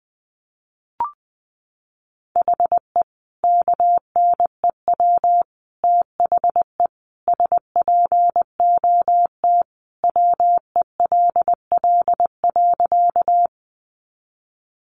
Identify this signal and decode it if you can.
Morse code, heard as a keyed tone, about 20 wpm, 700 Hz. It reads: HE KNEW THE SPOT WELL.
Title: Sentences from Top 1100 Words 20wpm